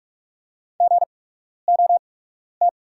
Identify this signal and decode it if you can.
Morse code, heard as a keyed tone, about 44 wpm, spaced as for 12 wpm, 690 Hz. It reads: GOT
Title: 8x Character Spacing - Top 500 Words - 45wpm